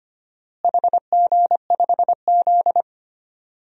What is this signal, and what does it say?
Morse code, heard as a keyed tone, about 25 wpm, 700 Hz. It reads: HG5Z
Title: DXCC-100 Call Signs - 4 Characters - Repeated 2x - 25wpm